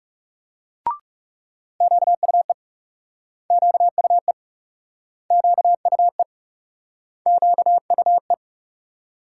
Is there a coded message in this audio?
Yes: Morse code from 1.8 to 8.3 s, heard as a keyed tone, about 38 wpm, 700 Hz. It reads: QUE QUE QUE QUE